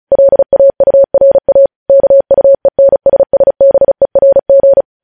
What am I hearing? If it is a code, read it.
Morse code, heard as a keyed tone, about 36 wpm, 550 Hz. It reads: LAURA KUENSSBERG
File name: podcast / News headlines-1x-35x35